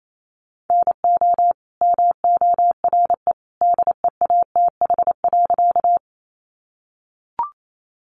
Morse code, heard as a keyed tone, about 28 wpm, 700 Hz. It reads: NO MORE DEATH.
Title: Sentences from Top 1100 Words 28wpm